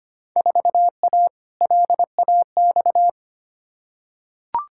Morse code, heard as a keyed tone, about 25 wpm, 700 Hz. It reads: 4A LAX
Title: Field Day Exchanges - 25wpm